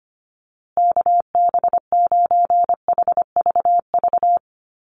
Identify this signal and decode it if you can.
Morse code, heard as a keyed tone, about 25 wpm, 700 Hz. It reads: KB9HVV
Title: USA Call Signs Repeated 3x 25wpm